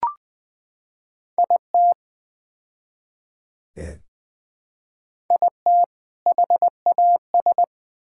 Morse code, heard as a keyed tone, about 20 wpm, 700 Hz. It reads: IT IT HAS
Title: Build A Phrase - 4-Word Phrases - 20wpm